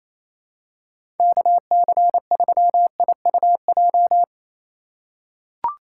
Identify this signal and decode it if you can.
Morse code, heard as a keyed tone, about 28 wpm, 700 Hz. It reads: KC3IUJ
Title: USA Call Signs - 2x Character Spacing - 28wpm